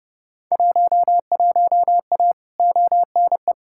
Morse code, heard as a keyed tone, about 32 wpm, 700 Hz. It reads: 11A ONE
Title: Field Day Exchanges Encore - 30wpm